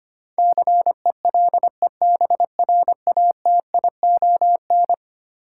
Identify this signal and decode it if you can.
Morse code, heard as a keyed tone, about 25 wpm, 700 Hz. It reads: CELEBRATION
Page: morse-code-ninja-practice-25wpm